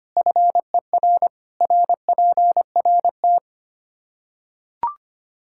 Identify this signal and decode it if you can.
Morse code, heard as a keyed tone, about 25 wpm, 700 Hz. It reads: FER RPRT
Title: Practice QSO - Exchange 2 - Info - 25wpm